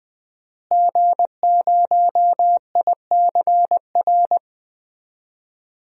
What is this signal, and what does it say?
Morse code, heard as a keyed tone, about 20 wpm, 700 Hz. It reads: G0ICR
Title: DXCC-100 Call Signs - 5 Characters - Repeated 3x - 20wpm